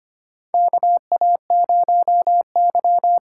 Morse code, heard as a keyed tone, about 25 wpm, 700 Hz. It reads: KA0Y
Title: USA Call Signs - 6 Characters - 6x Character Spacing - 25wpm